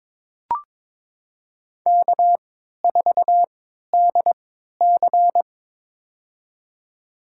Morse code, heard as a keyed tone, about 22 wpm, spaced as for 13 wpm, 700 Hz. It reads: K4DC
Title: USA Call Signs - 4 Characters - 3x Character Spacing - 22wpm